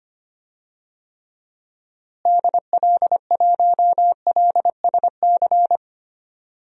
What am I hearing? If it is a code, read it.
Morse code, heard as a keyed tone, about 25 wpm, 700 Hz. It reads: DL1LSC
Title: DXCC-100 Call Signs - 6 Characters - Repeated 2x - 25wpm